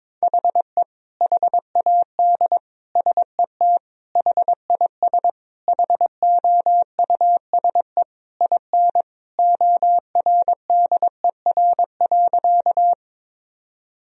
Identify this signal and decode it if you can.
Morse code, heard as a keyed tone, about 22 wpm, 700 Hz. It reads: HE HAD SET HIS HOUSE IN ORDER.
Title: Sentences from Top 1000 Words - Review Part 2 - 22wpm